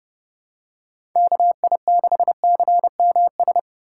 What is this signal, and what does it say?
Morse code, heard as a keyed tone, about 30 wpm, 700 Hz. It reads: KI6CMS
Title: USA Call Signs - 2x Character Spacing - 30wpm